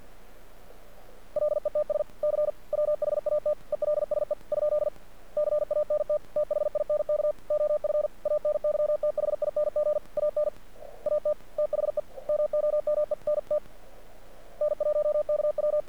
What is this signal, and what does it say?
Morse code, heard as a keyed tone, about 48 wpm, 600 Hz. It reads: LETS C WHAT ELSE ? CANT THINK OF ANYTHING AN AT THE MOMENT N1KW
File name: archived / CRSnet